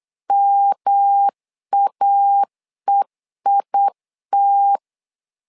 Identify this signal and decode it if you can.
Morse code, heard as a keyed tone, about 8 wpm, 800 Hz. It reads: MAEIT
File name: LEZIONI / 3 MWRO / 3 M 8wpm 2300